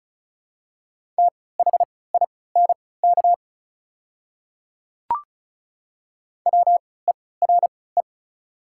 Morse code, heard as a keyed tone, about 34 wpm, spaced as for 20 wpm, 700 Hz. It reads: THINK WERE
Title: 3x Character Spacing - Top 200 Words - 35wpm